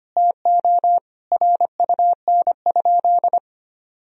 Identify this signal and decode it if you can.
Morse code, heard as a keyed tone, about 25 wpm, 700 Hz. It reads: TO RUN?
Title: Sentences from Top 1000 Words - Review Part 2 - 25wpm